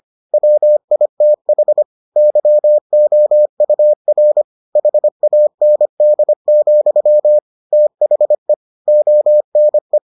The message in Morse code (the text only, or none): WITH YOUR HAND, THE ONE